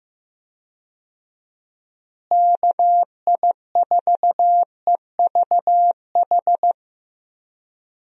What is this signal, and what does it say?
Morse code, heard as a keyed tone, about 16 wpm, 700 Hz. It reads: KI4EVH
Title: USA Call Signs - 6 Characters - Repeated 3x - 15wpm